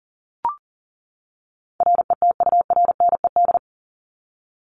Morse code, heard as a keyed tone, about 40 wpm, 700 Hz. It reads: RETURNED
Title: ED Suffix - Top 100 Words with Base Words 40wpm